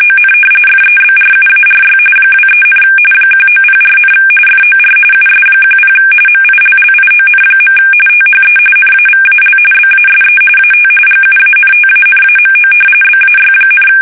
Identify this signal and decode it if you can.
RTTY, about 45.45 baud. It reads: ON FEBRUARY 23, 1904, FOR US10 MILLION. THE NAVAL BASE AT GUANTANAMO BAY,
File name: fsk850 / usnavy3